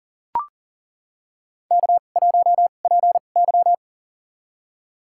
Morse code, heard as a keyed tone, about 37 wpm, spaced as for 28 wpm, 700 Hz. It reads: K1PY
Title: USA Call Signs - 4 Characters - 2x Character Spacing - 40wpm